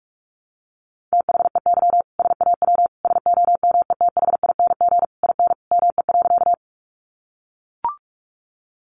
Morse code, heard as a keyed tone, about 44 wpm, 700 Hz. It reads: THEY SAW SOMETHING IN ME.
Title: Sentences from Top 600 Words 45wpm